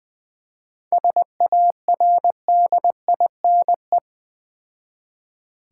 Morse code, heard as a keyed tone, about 20 wpm, 690 Hz. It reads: SARDINE
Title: Difficult Words with Hints - 20wpm